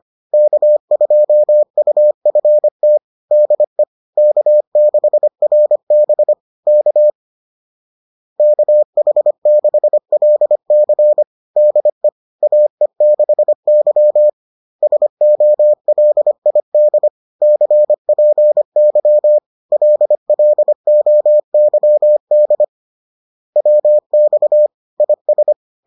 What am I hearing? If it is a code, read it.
Morse code, heard as a keyed tone, about 25 wpm, 600 Hz. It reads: K2UFT DE K6RB K KH6LC DE AE6Y SOLID CPY LLOYD WX IS